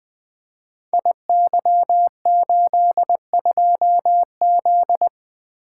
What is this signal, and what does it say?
Morse code, heard as a keyed tone, about 20 wpm, 700 Hz. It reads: IY82Z
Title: Code Groups - 5 Letter:Numbers - 20wpm